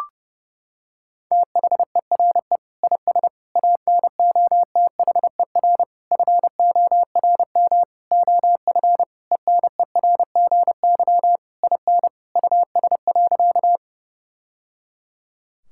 Morse code, heard as a keyed tone, about 30 wpm, 700 Hz. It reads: THERE IS ANOTHER FORM OF ENERGY IN US.
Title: Sentences from Top 1000 Words - Review Part 6 - 30wpm